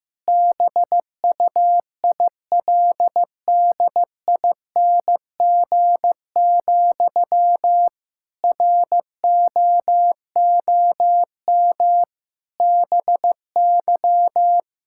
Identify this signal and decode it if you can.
Morse code, heard as a keyed tone, about 15 wpm, 700 Hz. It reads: BUILDING, ROOM BY
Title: Sentences from Top 1000 Words - Review Part 5 - 15wpm